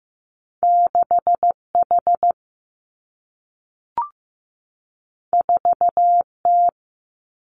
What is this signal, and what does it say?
Morse code, heard as a keyed tone, about 15 wpm, 700 Hz. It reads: 6 H 4 T
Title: Two Challenging Letter-Number Combinations 15wpm